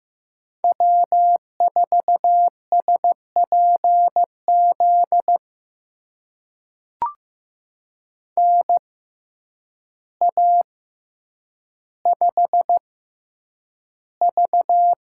Morse code, heard as a keyed tone, about 15 wpm, spaced as for 8 wpm, 700 Hz. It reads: W4SPZ NA5V
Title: USA Call Signs - 6x Character Spacing - 15wpm